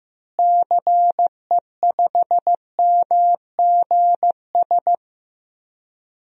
Morse code, heard as a keyed tone, about 15 wpm, 700 Hz. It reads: CE5MGS
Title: DXCC-100 Call Signs - 6 Characters - Repeated 3x - 15wpm